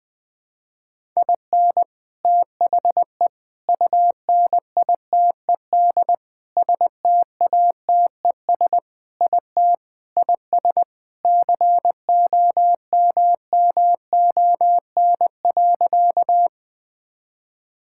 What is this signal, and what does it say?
Morse code, heard as a keyed tone, about 20 wpm, 700 Hz. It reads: IN THE UNITED STATES IT IS COMMON.
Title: Sentences with 4-Word Phrases - Spoken Beginning - 20wpm